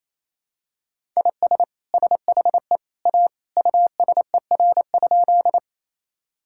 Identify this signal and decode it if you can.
Morse code, heard as a keyed tone, about 27 wpm, 700 Hz. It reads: IS SHE A USER?
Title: Sentences from Top 1000 Words - Review Part 5 - 28wpm